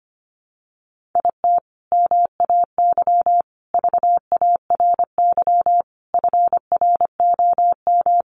Morse code, heard as a keyed tone, about 25 wpm, 700 Hz. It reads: IT MAY VARY FROM
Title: Sentences from Top 1200 Words 25wpm